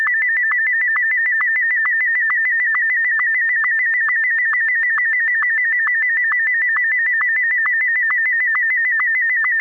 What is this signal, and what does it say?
SSTV (slow-scan television), Martin 1.